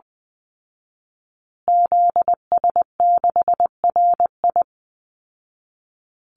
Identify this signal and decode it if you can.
Morse code, heard as a keyed tone, about 20 wpm, 700 Hz. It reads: ZS6RI
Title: Call Signs Repeated 3x - 20wpm